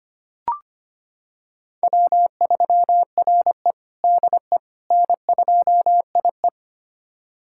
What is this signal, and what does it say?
Morse code, heard as a keyed tone, about 25 wpm, 700 Hz. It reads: W3RE DE N2IE
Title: USA Callsign DE Callsign - 4 Characters - 25wpm